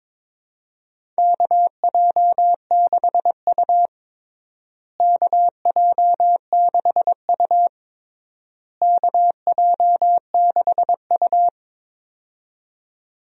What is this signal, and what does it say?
Morse code, heard as a keyed tone, about 21 wpm, 700 Hz. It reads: KJ6U KJ6U KJ6U
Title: USA Call Signs - 4 Characters - Repeated 3x - 22wpm